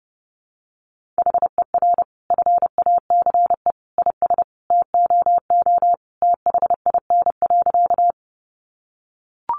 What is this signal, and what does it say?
Morse code, heard as a keyed tone, about 28 wpm, 700 Hz. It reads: HER FACE IS TOO THIN.
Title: Sentences from Top 1200 Words 30wpm